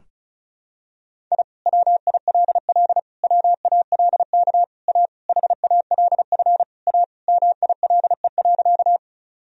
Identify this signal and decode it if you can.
Morse code, heard as a keyed tone, about 36 wpm, 700 Hz. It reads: I WILL WALK A HALF A MILE.
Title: Sentences from Top 1000 Words - Review Part 1 - 35wpm